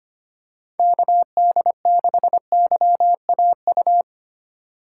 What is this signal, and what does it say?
Morse code, heard as a keyed tone, about 25 wpm, 700 Hz. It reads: KD6YAU
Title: USA Call Signs - 2x Character Spacing - 25wpm